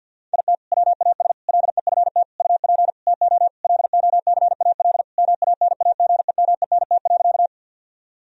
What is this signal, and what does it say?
Morse code, heard as a keyed tone, about 50 wpm, 700 Hz. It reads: IT WAS LEFT UP TO LOCAL MANAGEMENT.
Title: Sentences from Top 1000 Words - Review Part 4 - 50wpm